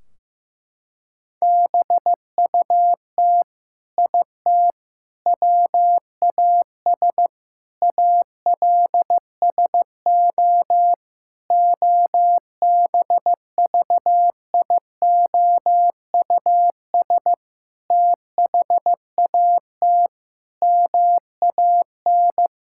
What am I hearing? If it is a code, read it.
Morse code, heard as a keyed tone, about 15 wpm, 700 Hz. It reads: BUT IT WAS ALSO OBVIOUS THAT MAN
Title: Sentences from Top 1600 Words 15wpm